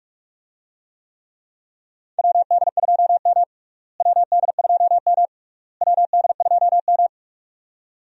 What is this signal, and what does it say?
Morse code, heard as a keyed tone, about 46 wpm, 700 Hz. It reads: WD2M WD2M WD2M
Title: USA Call Signs - 4 Characters - Repeated 3x - 45wpm